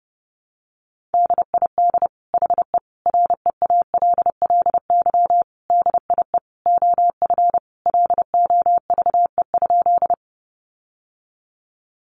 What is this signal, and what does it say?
Morse code, heard as a keyed tone, about 30 wpm, 700 Hz. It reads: DID HE REALLY DIE OF LOVE?
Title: Sentences from Top 1300 Words 30wpm